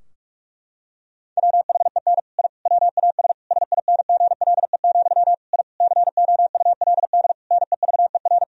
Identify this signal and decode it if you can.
Morse code, heard as a keyed tone, about 41 wpm, 700 Hz. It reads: WHEN I WAS SINGLE, I COULD NEVER
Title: Sentences from Top 1000 Words - Review Part 4 - 45wpm